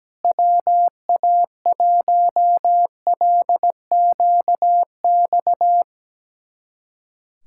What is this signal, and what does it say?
Morse code, heard as a keyed tone, about 17 wpm, 700 Hz. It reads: WA1LQX
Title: USA Callsign DE Callsign - 6 Characters - 17wpm